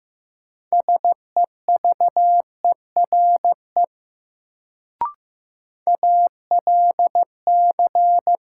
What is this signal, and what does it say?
Morse code, heard as a keyed tone, about 15 wpm, 700 Hz. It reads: SEVERE ALC
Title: Sentences from Top 1400 Words - New Word Drill - 15wpm